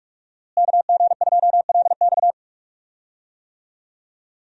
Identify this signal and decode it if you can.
Morse code, heard as a keyed tone, about 41 wpm, 700 Hz. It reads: KG2LX
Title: USA Call Signs - 5 Characters - Repeated 3x - 45wpm